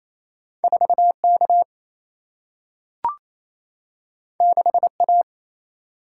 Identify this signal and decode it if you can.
Morse code, heard as a keyed tone, about 28 wpm, 700 Hz. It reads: 4K 6A
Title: Single-Letter Single-Number 28wpm